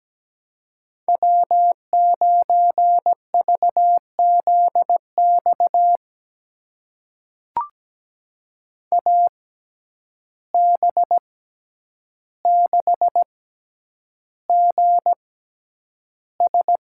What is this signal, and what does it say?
Morse code, heard as a keyed tone, about 17 wpm, spaced as for 9 wpm, 700 Hz. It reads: W9VZX AB6GS
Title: USA Call Signs - 6x Character Spacing - 17wpm